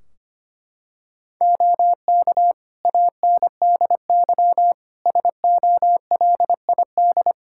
Morse code, heard as a keyed tone, about 25 wpm, 700 Hz. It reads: OK ANDY SOLID
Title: K1USN SST QSO Practice 25wpm